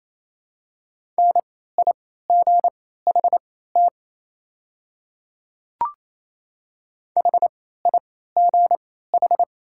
Morse code, heard as a keyed tone, about 27 wpm, spaced as for 16 wpm, 700 Hz. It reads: NIGHT HIGH
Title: 3x Character Spacing - Top 300 Words - 28wpm